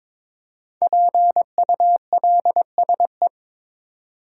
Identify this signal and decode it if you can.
Morse code, heard as a keyed tone, about 22 wpm, 690 Hz. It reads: PULSE
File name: 5 Letter Words - First 3 Letters Given - 22wpm